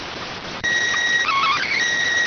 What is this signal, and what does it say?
SSTV (slow-scan television), PD 290.